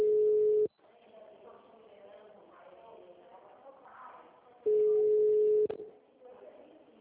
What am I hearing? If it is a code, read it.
Ringback tone (Europe, 425 Hz), with speech.